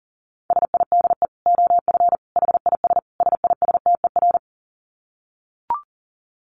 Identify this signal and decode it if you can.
Morse code, heard as a keyed tone, about 40 wpm, 700 Hz. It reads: SIDE OF HIS SISTER
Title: Dit Challenge 40wpm